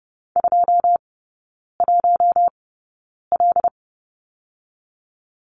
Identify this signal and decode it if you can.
Morse code, heard as a keyed tone, about 30 wpm, spaced as for 9 wpm, 700 Hz. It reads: 21L
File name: Lesson 18e - Letter m - Three Characters 30wpm